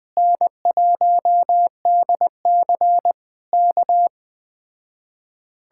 Morse code, heard as a keyed tone, about 20 wpm, 700 Hz. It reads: N1DC K
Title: Speed-Racing Common QSO Phrases - 17wpm